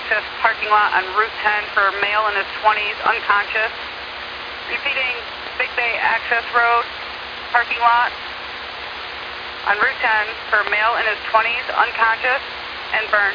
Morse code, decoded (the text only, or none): none